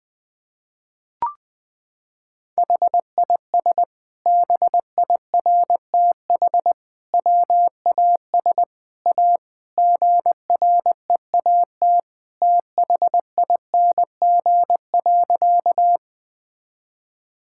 Morse code, heard as a keyed tone, about 20 wpm, 700 Hz. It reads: HIS BIRTH WAS A GREAT THING.
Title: Sentences from Top 1000 Words - Review Part 5 - 20wpm